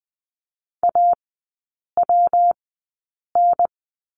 Morse code, heard as a keyed tone, about 19 wpm, 700 Hz. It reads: A W N
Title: 114 Lesson 19e - Letter w - Three Characters 20wpm